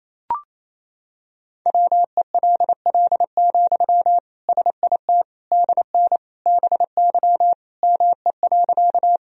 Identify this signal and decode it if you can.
Morse code, heard as a keyed tone, about 28 wpm, 700 Hz. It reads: WELL, SIT DN BY ME.
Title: Sentences from Top 500 Words and Abbreviations 28wpm